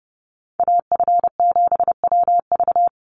Morse code, heard as a keyed tone, about 28 wpm, 700 Hz. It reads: AF7WV
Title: USA Call Signs 30wpm